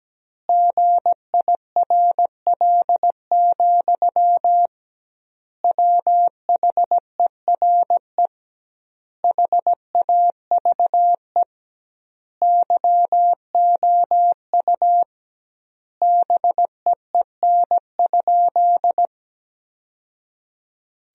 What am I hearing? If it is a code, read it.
Morse code, heard as a keyed tone, about 17 wpm, 700 Hz. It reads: GIRL, WHERE HAVE YOU BEEN?
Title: 2x Word-Spacing - Sentences from Top 400 Words - 17wpm